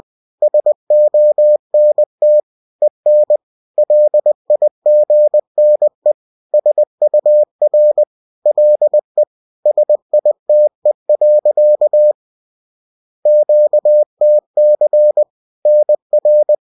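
Morse code, heard as a keyed tone, about 20 wpm, 600 Hz. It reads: SONT EN LIGNE SUR LE SITE. QTC NR
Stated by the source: Bulletin sent in CW.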